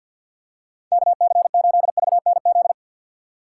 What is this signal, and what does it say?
Morse code, heard as a keyed tone, about 52 wpm, 700 Hz. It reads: KC8VNZ
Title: USA Call Signs - 6 Characters - 3x Character Spacing - 50wpm